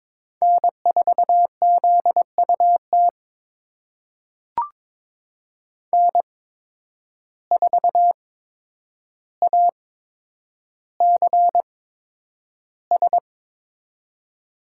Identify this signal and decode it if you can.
Morse code, heard as a keyed tone, about 22 wpm, spaced as for 9 wpm, 700 Hz. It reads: N4ZUT N4ACS
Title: USA Call Signs - 5 Characters - 8x Character Spacing - 22wpm